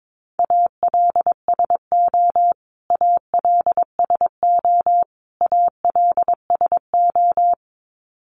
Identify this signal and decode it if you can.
Morse code, heard as a keyed tone, about 22 wpm, 700 Hz. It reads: ALSO ALSO ALSO
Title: Top 500 Words Repeated 22wpm